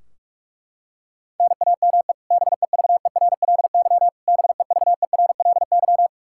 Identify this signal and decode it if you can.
Morse code, heard as a keyed tone, about 46 wpm, 700 Hz. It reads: NAME BEVERLY BEVERLY
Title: Practice QSO - Exchange 1 - Beginning - 45wpm